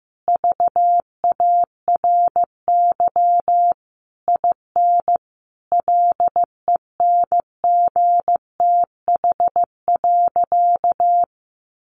Morse code, heard as a keyed tone, about 15 wpm, 700 Hz. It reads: VARY IN LENGTH.
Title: Sentences from Top 1200 Words 15wpm